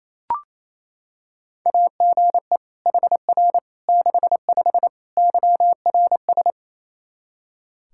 Morse code, heard as a keyed tone, about 27 wpm, 700 Hz. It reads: AGE HR 65 YRS
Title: Practice QSO - Exchange 3 - More Info - 28wpm